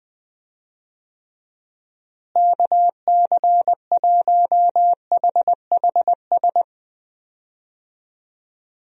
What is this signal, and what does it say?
Morse code, heard as a keyed tone, about 21 wpm, 700 Hz. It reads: KC1HHS
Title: USA Call Signs - 6 Characters - Repeated 3x - 20wpm